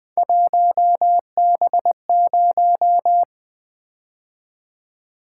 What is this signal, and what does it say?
Morse code, heard as a keyed tone, about 20 wpm, 700 Hz. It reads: 1B0
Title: Three Letter-Number Combinations - 20wpm